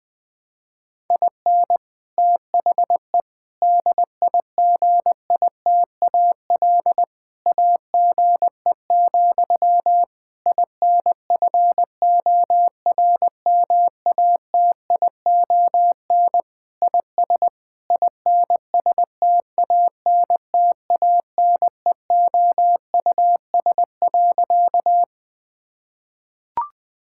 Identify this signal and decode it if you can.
Morse code, heard as a keyed tone, about 19 wpm, 700 Hz. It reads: IN THE DIGITAL AGE, INFORMATION IS INSTANTANEOUS.
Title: Difficult Sentences with Hints - 20wpm